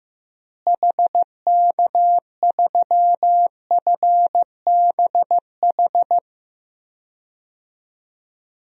Morse code, heard as a keyed tone, about 15 wpm, 700 Hz. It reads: HK3FBH
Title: DXCC-100 Call Signs - 6 Characters - Repeated 2x - 15wpm